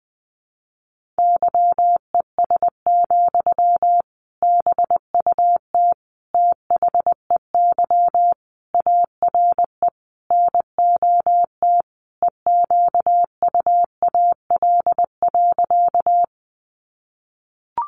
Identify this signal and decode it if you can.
Morse code, heard as a keyed tone, about 19 wpm, 700 Hz. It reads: YES, BUT THEY ARE NOT EQUAL.
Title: Sentences from Top 1400 Words 20wpm